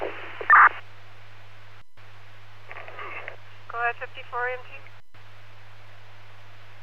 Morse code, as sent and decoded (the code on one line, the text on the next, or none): none